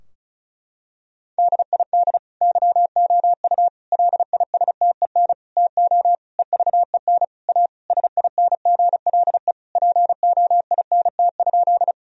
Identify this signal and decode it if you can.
Morse code, heard as a keyed tone, about 36 wpm, 700 Hz. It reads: DID YOU LISTEN TO EVEN A SINGLE POINT?
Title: Sentences from Top 1000 Words - Review Part 4 - 35wpm